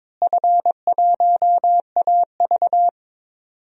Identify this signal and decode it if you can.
Morse code, heard as a keyed tone, about 21 wpm, 700 Hz. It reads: F1AV
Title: DXCC-100 Call Signs - 4 Characters - Repeated 3x - 22wpm